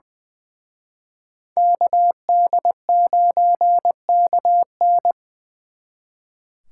Morse code, heard as a keyed tone, about 25 wpm, 700 Hz. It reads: KD9KN
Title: Speed Racing - USA 2x2 Call Signs 20wpm